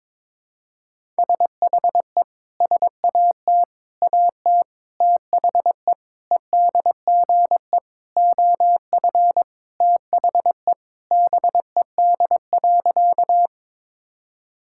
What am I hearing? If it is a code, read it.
Morse code, heard as a keyed tone, about 21 wpm, 700 Hz. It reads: SHE SAT AT THE EDGE OF THE BED.
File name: practice / 22wpm / Sentences with 4-Word Phrases - 22wpm